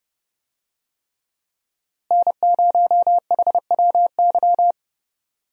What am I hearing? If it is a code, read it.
Morse code, heard as a keyed tone, about 32 wpm, 700 Hz. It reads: N0HWY
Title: USA Call Signs - 5 Characters - Repeated 3x - 30wpm